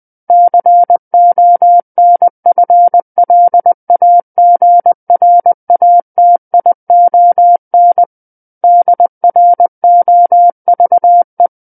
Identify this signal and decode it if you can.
Morse code, heard as a keyed tone, about 20 wpm, 700 Hz. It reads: CONFLAGRATION DROVE
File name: 20 WPM